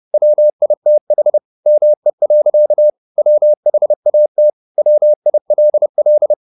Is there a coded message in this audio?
Yes: Morse code from 0.1 to 6.3 s, heard as a keyed tone, about 30 wpm, 600 Hz. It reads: WITH ME. WHAT WILL